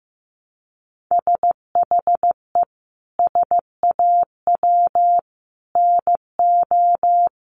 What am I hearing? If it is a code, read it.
Morse code, heard as a keyed tone, about 14 wpm, 700 Hz. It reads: SHE SAW NO
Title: Sentences from Top 300 Words 15wpm